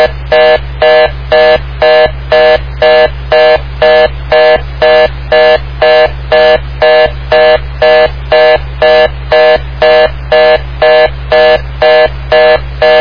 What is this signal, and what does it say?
Reorder tone (North America), 480 + 620 Hz, with noise.
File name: Long Lake Fire 2020 07 27 12 19 34